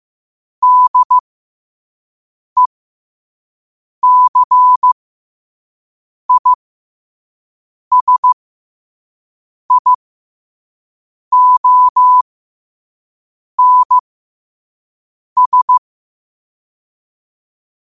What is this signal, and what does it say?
Morse code, heard as a keyed tone, about 15 wpm, spaced as for 5 wpm, 1000 Hz. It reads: DECISIONS
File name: qotdfiles / 5wpm